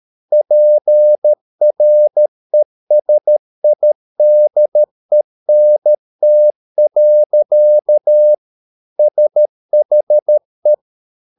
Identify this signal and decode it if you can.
Morse code, heard as a keyed tone, about 13 wpm, 600 Hz. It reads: PRESIDENT. SHE